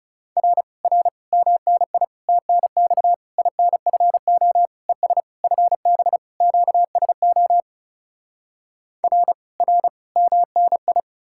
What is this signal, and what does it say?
Morse code, heard as a keyed tone, about 40 wpm, 700 Hz. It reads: R R MNI TNX INFO ES FB QSO R R MNI